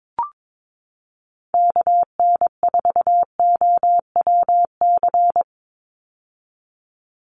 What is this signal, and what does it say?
Morse code, heard as a keyed tone, about 22 wpm, 700 Hz. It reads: KN4OWC
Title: USA Call Signs 22wpm